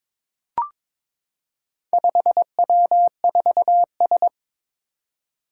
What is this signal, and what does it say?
Morse code, heard as a keyed tone, about 22 wpm, 700 Hz. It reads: 5W4S
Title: Four Challenging Letter-Number Combinations 22wpm